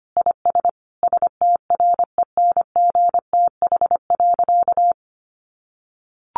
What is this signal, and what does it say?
Morse code, heard as a keyed tone, about 25 wpm, 700 Hz. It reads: IS STRENGTH.
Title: Sentences from Top 1400 Words 25wpm